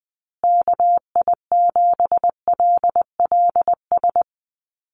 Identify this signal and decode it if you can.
Morse code, heard as a keyed tone, about 20 wpm, 700 Hz. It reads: KI7LLS
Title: USA Call Signs Repeated 3x 20wpm